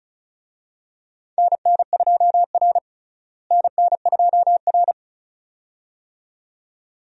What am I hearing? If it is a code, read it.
Morse code, heard as a keyed tone, about 34 wpm, 700 Hz. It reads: NN2R NN2R
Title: USA Call Signs - 4 Characters - Repeated 2x - 35wpm